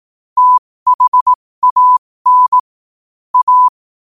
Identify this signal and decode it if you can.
Morse code, heard as a keyed tone, about 18 wpm, spaced as for 15 wpm, 1000 Hz. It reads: THAN A
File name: qotdfiles / 15wpm